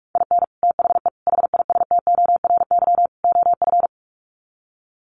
Morse code, heard as a keyed tone, about 44 wpm, 700 Hz. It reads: IN THE HISTORY OF
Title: Sets of 4 Words 45wpm